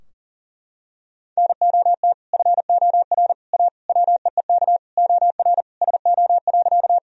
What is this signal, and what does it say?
Morse code, heard as a keyed tone, about 40 wpm, 700 Hz. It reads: NOT FOR A WEEK OR SO.
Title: Sentences from Top 1000 Words - Review Part 2 - 40wpm